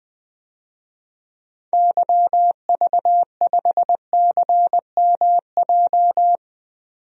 Morse code, heard as a keyed tone, about 20 wpm, 700 Hz. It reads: YV5CMJ
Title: DXCC-100 Call Signs - 6 Characters - 20wpm